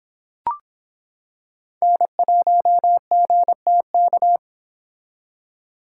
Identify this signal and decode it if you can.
Morse code, heard as a keyed tone, about 26 wpm, 700 Hz. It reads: N1GTK